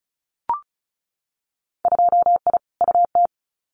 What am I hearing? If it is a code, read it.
Morse code, heard as a keyed tone, about 34 wpm, 700 Hz. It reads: 2I UT
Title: Winter Field Day Exchanges 35wpm